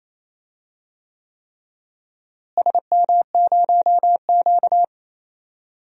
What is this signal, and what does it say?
Morse code, heard as a keyed tone, about 28 wpm, 700 Hz. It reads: SM0Q
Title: DXCC-100 Call Signs - 4 Characters - Repeated 3x - 28wpm